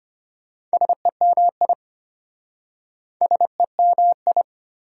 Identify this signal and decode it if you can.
Morse code, heard as a keyed tone, about 33 wpm, 690 Hz. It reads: SEMI SEMI